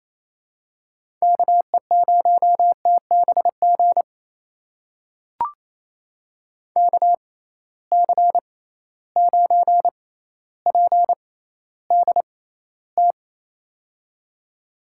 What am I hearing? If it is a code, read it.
Morse code, heard as a keyed tone, about 28 wpm, spaced as for 14 wpm, 700 Hz. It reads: KE0TBG KC9PDT
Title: USA Call Signs - 6 Characters - 6x Character Spacing - 28wpm